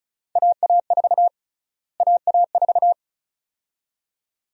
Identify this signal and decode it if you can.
Morse code, heard as a keyed tone, about 36 wpm, 700 Hz. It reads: AA4 AA4
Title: USA Call Sign Prefixes - Repeated 2x - 35wpm